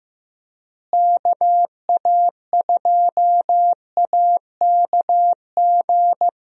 Morse code, heard as a keyed tone, about 15 wpm, 700 Hz. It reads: KA2AKG
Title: USA Call Signs - 6 Characters - Repeated 3x - 15wpm